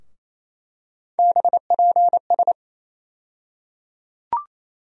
Morse code, heard as a keyed tone, about 28 wpm, 700 Hz. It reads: BPS
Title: USA Call Sign Postfixes - 3 Letters 28wpm